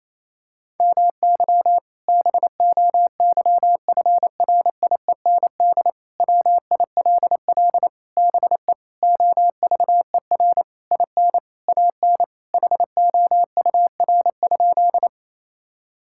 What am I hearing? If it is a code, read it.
Morse code, heard as a keyed tone, about 28 wpm, 700 Hz. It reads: MY BOYFRIEND WILL BE OVER IN AN HOUR?
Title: Sentences from Top 1000 Words - Review Part 5 - 28wpm